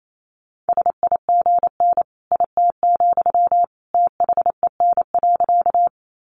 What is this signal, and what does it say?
Morse code, heard as a keyed tone, about 28 wpm, 700 Hz. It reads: SIGN IT, THEN.
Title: Sentences from Top 1000 Words 28wpm